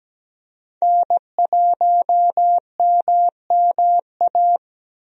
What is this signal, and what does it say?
Morse code, heard as a keyed tone, about 17 wpm, 700 Hz. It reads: N1MMA
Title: USA Call Signs - 5 Characters - Repeated 2x - 17wpm